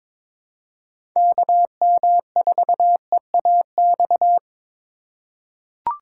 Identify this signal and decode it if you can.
Morse code, heard as a keyed tone, about 22 wpm, 700 Hz. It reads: KM4EAX
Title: USA Call Signs - 6 Characters - 4x Character Spacing - 22wpm